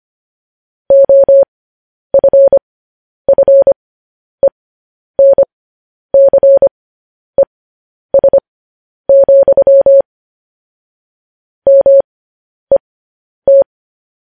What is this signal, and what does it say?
Morse code, heard as a keyed tone, about 25 wpm, spaced as for 10 wpm, 550 Hz. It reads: OFFENCES, MET
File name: podcast / News headlines-1x-25x10